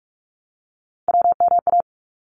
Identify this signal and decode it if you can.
Morse code, heard as a keyed tone, about 44 wpm, 700 Hz. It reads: WMA
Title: ARRL Sections 45wpm